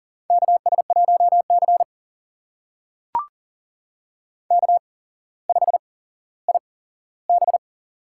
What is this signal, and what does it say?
Morse code, heard as a keyed tone, about 40 wpm, spaced as for 17 wpm, 700 Hz. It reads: KS1C K5IB
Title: USA Call Signs - 4 Characters - 8x Character Spacing - 40wpm